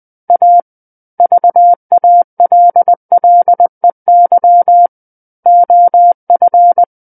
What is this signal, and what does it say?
Morse code, heard as a keyed tone, about 20 wpm, 700 Hz. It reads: A VALLEY OF
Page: SKCC Learning Center - Morse Code Practice Files